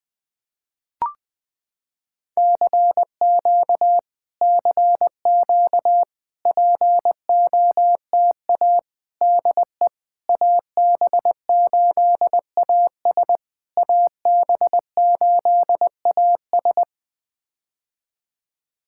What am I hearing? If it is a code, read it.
Morse code, heard as a keyed tone, about 20 wpm, 700 Hz. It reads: CQ CQ POTA DE AB8AS AB8AS
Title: POTA QSO Practice - 20wpm